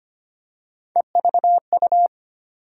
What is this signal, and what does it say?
Morse code, heard as a keyed tone, about 25 wpm, 700 Hz. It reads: EVU
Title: USA Call Sign Postfixes - 3 Letters 25wpm